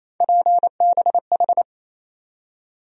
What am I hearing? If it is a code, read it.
Morse code, heard as a keyed tone, about 28 wpm, 700 Hz. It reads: PBH
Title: USA Call Sign Postfixes - 3 Letters 28wpm